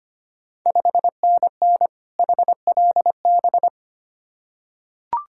Morse code, heard as a keyed tone, about 25 wpm, 690 Hz. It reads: 5NN HLB
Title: Ontario QSO Party Exchange 25wpm